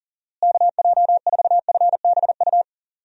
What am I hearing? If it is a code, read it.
Morse code, heard as a keyed tone, about 40 wpm, 700 Hz. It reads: KJ4FBU
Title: USA Call Signs - 6 Characters - Repeated 2x - 40wpm